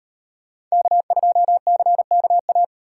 Morse code, heard as a keyed tone, about 38 wpm, 700 Hz. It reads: K2CKA